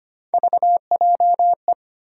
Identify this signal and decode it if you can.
Morse code, heard as a keyed tone, about 25 wpm, 700 Hz. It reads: VJE
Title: USA Call Sign Postfixes - 3 Letters 25wpm